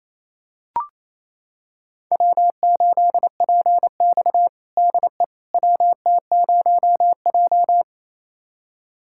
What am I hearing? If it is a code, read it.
Morse code, heard as a keyed tone, about 28 wpm, 700 Hz. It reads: W8PX DE WT0J
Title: USA Callsign DE Callsign - 4 Characters - 28wpm